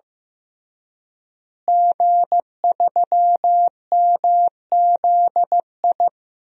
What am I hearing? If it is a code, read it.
Morse code, heard as a keyed tone, about 16 wpm, 700 Hz. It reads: G3MZI
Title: DXCC-100 Call Signs - 5 Characters - Repeated 3x - 15wpm